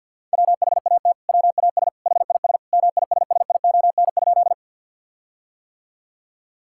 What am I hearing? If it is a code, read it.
Morse code, heard as a keyed tone, about 51 wpm, 700 Hz. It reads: WHAT WAS HIS MISSION?
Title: Sentences from Top 1600 Words 50wpm